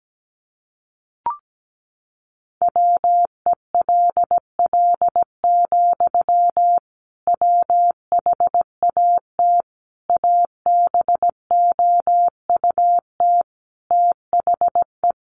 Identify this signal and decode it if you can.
Morse code, heard as a keyed tone, about 17 wpm, 700 Hz. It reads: WELL, WHAT ABOUT THE
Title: Sentences from Top 1200 Words 17wpm